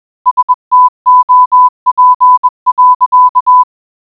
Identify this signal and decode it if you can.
Morse code, heard as a keyed tone, about 20 wpm, 1000 Hz. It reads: STOP.